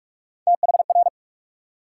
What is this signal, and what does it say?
Morse code, heard as a keyed tone, about 60 wpm, 700 Hz. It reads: THR